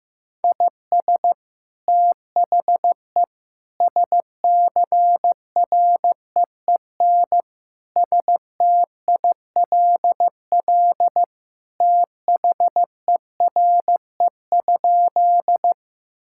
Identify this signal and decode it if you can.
Morse code, heard as a keyed tone, about 15 wpm, 700 Hz. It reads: IS THE SCREEN STILL THERE?
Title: Sentences from Top 1000 Words - Review Part 6 - 15wpm